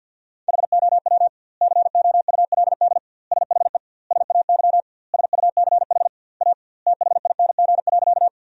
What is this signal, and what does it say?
Morse code, heard as a keyed tone, about 50 wpm, 700 Hz. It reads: HOW COULD SHE SAY SUCH A THING.
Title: 2x Word-Spacing - Sentences from Top 200 Words - 50wpm